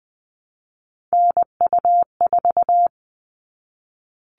Morse code, heard as a keyed tone, about 20 wpm, 700 Hz. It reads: NU4
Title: All Cut Numbers - 3-Digit 20wpm